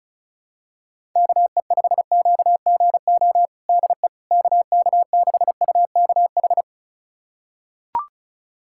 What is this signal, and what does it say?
Morse code, heard as a keyed tone, about 34 wpm, 700 Hz. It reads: KE5QGO DE KK6UKH
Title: USA Callsign DE Callsign - 6 Characters - 35wpm